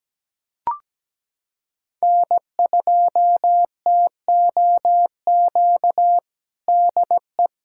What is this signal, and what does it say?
Morse code, heard as a keyed tone, about 17 wpm, 700 Hz. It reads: N2TOQ DE
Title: USA Callsign DE Callsign - 5 Characters - 17wpm